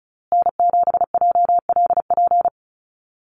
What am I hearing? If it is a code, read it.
Morse code, heard as a keyed tone, about 34 wpm, 700 Hz. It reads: N7JLP
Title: USA Call Signs Repeated 2x 35wpm